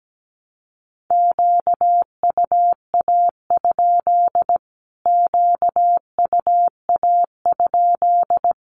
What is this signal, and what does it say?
Morse code, heard as a keyed tone, about 17 wpm, 700 Hz. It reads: QUA? QUA?
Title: Q-Codes Repeated 3x 17wpm